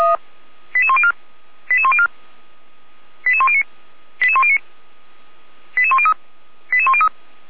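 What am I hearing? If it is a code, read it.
Five-tone selective call (ZVEI), heard as quick runs of five tones, tones of about 70 ms. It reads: 88104 88104 88109 88109 88103 88103
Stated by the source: ZVEI five-tone selective call received with an SDR, from Feuerwehr Orlamünde.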